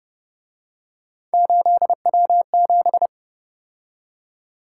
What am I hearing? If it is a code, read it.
Morse code, heard as a keyed tone, about 31 wpm, 700 Hz. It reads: 8W7
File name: Speed-Racing - Three Letter-Number Combinations - 25wpm